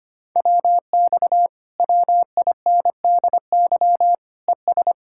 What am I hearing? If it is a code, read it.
Morse code, heard as a keyed tone, about 26 wpm, 700 Hz. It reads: WX WINDY ES
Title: Speed-Racing Common QSO Phrases - 25wpm